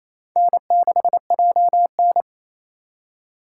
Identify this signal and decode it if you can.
Morse code, heard as a keyed tone, about 34 wpm, 700 Hz. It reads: N6JN
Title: Speed Racing - USA Call Signs 28wpm